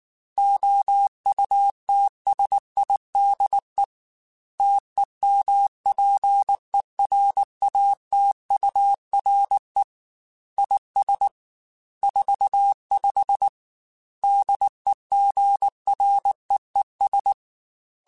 Morse code, heard as a keyed tone, about 19 wpm, 800 Hz. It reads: OUTSIDE TEMPERATURE IS 45 DEGREES
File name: Weather Sets / Weather Sets-mp3s / Weather 3 17 @18